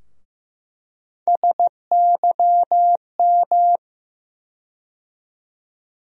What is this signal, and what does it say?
Morse code, heard as a keyed tone, about 15 wpm, 700 Hz. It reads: SYM
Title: USA Call Sign Postfixes - 3 Letters 15wpm